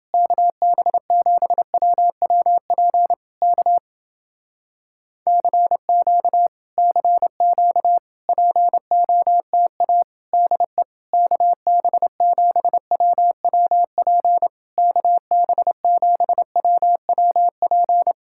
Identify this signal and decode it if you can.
Morse code, heard as a keyed tone, about 30 wpm, 700 Hz. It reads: KB7WWP K CQ CQ POTA DE KB7WWP KB7WWP